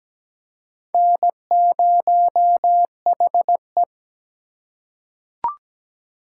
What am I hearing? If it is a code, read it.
Morse code, heard as a keyed tone, about 17 wpm, 700 Hz. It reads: N0HE
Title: USA Call Signs - 2x Character Spacing - 17wpm